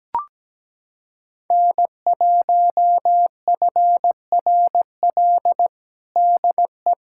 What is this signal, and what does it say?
Morse code, heard as a keyed tone, about 17 wpm, 700 Hz. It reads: N1FRL DE
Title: USA Callsign DE Callsign 17wpm